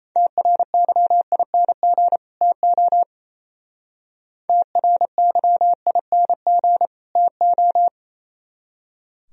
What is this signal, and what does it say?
Morse code, heard as a keyed tone, about 34 wpm, 700 Hz. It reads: TRYING TO TRYING TO
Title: Speed-Racing - Sets of 2 Words 28wpm